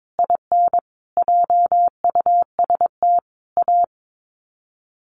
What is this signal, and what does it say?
Morse code, heard as a keyed tone, about 22 wpm, 700 Hz. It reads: IN JUST A
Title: Sets of 5 Words Encore 22wpm